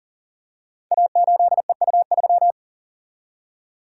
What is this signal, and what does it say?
Morse code, heard as a keyed tone, about 38 wpm, 700 Hz. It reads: A8EU3
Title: Code Groups - 5 Letter:Numbers - 40wpm